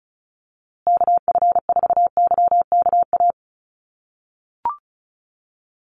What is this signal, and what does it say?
Morse code, heard as a keyed tone, about 34 wpm, 700 Hz. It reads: KF4YKA
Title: USA Call Signs Repeated 2x 35wpm